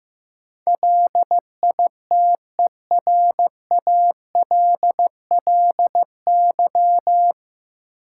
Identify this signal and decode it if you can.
Morse code, heard as a keyed tone, about 16 wpm, 700 Hz. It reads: LITERALLY
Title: Sentences from Top 1400 Words - New Word Drill Alt - 15wpm